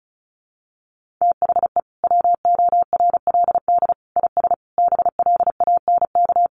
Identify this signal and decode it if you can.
Morse code, heard as a keyed tone, about 34 wpm, 700 Hz. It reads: THE WORLD IS BLANK
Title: Sentences from Top 1500 Words 35wpm